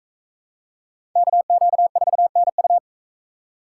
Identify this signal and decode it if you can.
Morse code, heard as a keyed tone, about 38 wpm, 700 Hz. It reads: KQ4NU